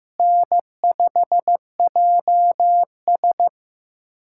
Morse code, heard as a keyed tone, about 15 wpm, 700 Hz. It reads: N5JS
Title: USA Callsign DE Callsign - 4 Characters - 15wpm